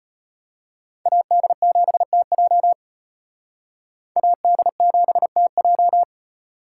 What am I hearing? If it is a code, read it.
Morse code, heard as a keyed tone, about 32 wpm, 700 Hz. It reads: AD7TJ AD7TJ